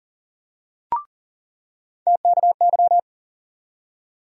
Morse code, heard as a keyed tone, about 40 wpm, 700 Hz. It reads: TKY
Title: USA Call Sign Postfixes - 3 Letters 40wpm